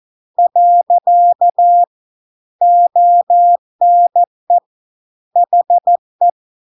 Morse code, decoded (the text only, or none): . ONE HE